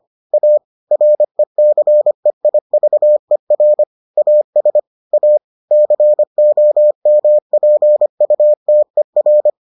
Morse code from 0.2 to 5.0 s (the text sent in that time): A RECEIVER AS